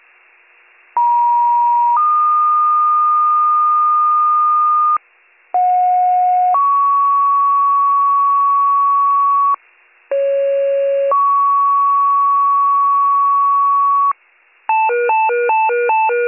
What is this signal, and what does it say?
Two-tone page, heard as a one-second tone, then a three-second tone, about 950 then 1220 Hz, about 730 then 1120 Hz, about 550 then 1120 Hz. It is heard from 1.0 to 14.1 s.